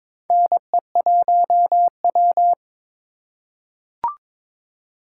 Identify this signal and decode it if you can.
Morse code, heard as a keyed tone, about 22 wpm, 700 Hz. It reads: NE1W
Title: USA Call Signs - 4 Characters - 3x Character Spacing - 22wpm